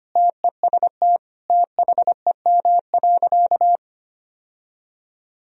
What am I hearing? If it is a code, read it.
Morse code, heard as a keyed tone, about 25 wpm, 700 Hz. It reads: TEST THEM.
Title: Sentences from Top 1000 Words - Review Part 3 - 25wpm